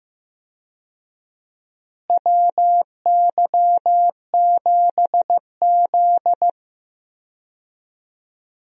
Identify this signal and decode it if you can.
Morse code, heard as a keyed tone, about 15 wpm, 700 Hz. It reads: WY7Z
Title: USA Call Signs - 4 Characters - Repeated 2x - 15wpm